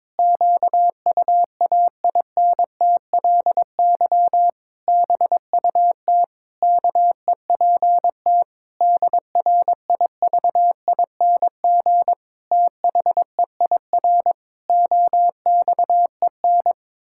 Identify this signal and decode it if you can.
Morse code, heard as a keyed tone, about 22 wpm, 700 Hz. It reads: QUAINTLY BUT KEPT DRIVING THEIR OXEN
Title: Pangram Challenge v2 - 22wpm